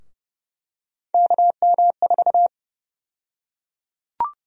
Morse code, heard as a keyed tone, about 30 wpm, 700 Hz. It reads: KM4
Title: USA Call Sign Prefixes - 30wpm